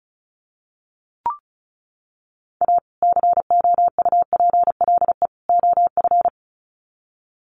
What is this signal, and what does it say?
Morse code, heard as a keyed tone, about 34 wpm, 700 Hz. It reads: A COUPLE OF
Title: Sets of 3 Words 35wpm